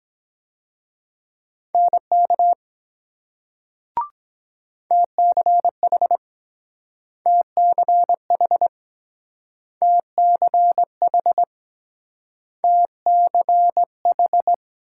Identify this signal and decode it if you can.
Morse code, heard as a keyed tone, about 23 wpm, 700 Hz. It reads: NK TCH TCH TCH TCH